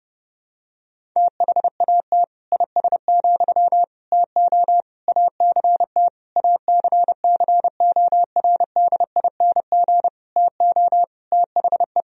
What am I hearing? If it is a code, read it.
Morse code, heard as a keyed tone, about 28 wpm, 700 Hz. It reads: THAT IS, TO ACT ACCORDING TO THE
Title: Sentences from Top 1000 Words - Review Part 4 - 30wpm